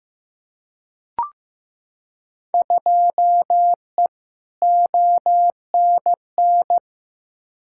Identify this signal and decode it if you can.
Morse code, heard as a keyed tone, about 16 wpm, 700 Hz. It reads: 2E ONN
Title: Field Day Exchanges Encore - 15wpm